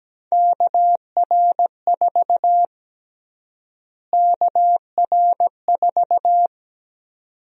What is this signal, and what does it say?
Morse code, heard as a keyed tone, about 17 wpm, 700 Hz. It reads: KR4 KR4
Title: USA Call Sign Prefixes - Repeated 2x - 17wpm